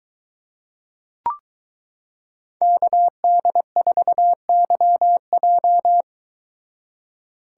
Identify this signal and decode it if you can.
Morse code, heard as a keyed tone, about 22 wpm, 700 Hz. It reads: KD4YJ